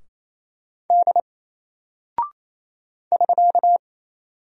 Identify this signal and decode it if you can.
Morse code, heard as a keyed tone, about 28 wpm, 700 Hz. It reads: D <SK>
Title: Single Character (Letter:Number:Punctuation:Prosign) Alt 28wpm